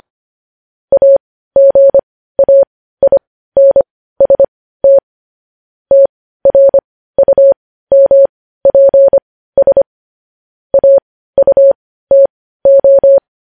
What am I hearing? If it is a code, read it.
Morse code, heard as a keyed tone, about 25 wpm, spaced as for 15 wpm, 550 Hz. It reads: AGAINST TRUMPS AUTO
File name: podcast / News headlines-1x-25x15